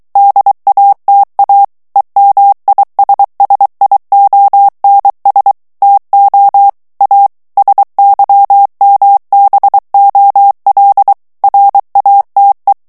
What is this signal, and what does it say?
Morse code, heard as a keyed tone, about 23 wpm, 800 Hz. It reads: DATA EMISSIONS TO A SYMBOL RATE
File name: ANGLAIS texte clair 3 vitesse 22 mots minute